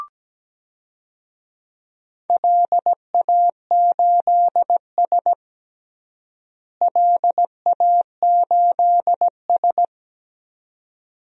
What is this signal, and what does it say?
Morse code, heard as a keyed tone, about 17 wpm, 700 Hz. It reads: LA8S LA8S
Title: DXCC-100 Call Signs - 4 Characters - Repeated 2x - 17wpm